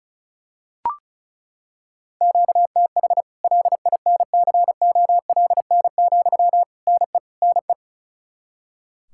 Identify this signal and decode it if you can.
Morse code, heard as a keyed tone, about 36 wpm, 700 Hz. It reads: QTH LINCOLN, NE NE
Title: Practice QSO - Exchange 1 - Beginning - 35wpm